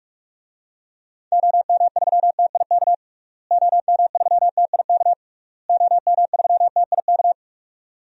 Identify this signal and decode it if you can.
Morse code, heard as a keyed tone, about 46 wpm, 700 Hz. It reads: OM3TIK OM3TIK OM3TIK
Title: DXCC-100 Call Signs - 6 Characters - Repeated 3x - 45wpm